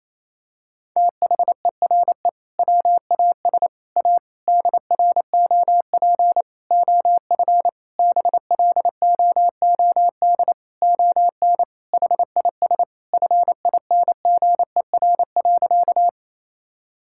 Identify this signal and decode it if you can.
Morse code, heard as a keyed tone, about 28 wpm, 700 Hz. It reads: THERE WAS A DROP OF BLOOD ON HIS FINGER.
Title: Sentences from Top 1600 Words 28wpm